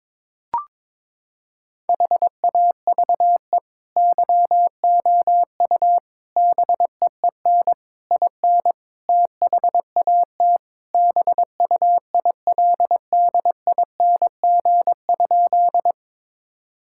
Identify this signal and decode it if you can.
Morse code, heard as a keyed tone, about 22 wpm, 700 Hz. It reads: HAVE YOU BEEN IN THAT BUILDING?
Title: Sentences from Top 1000 Words - Review Part 5 - 22wpm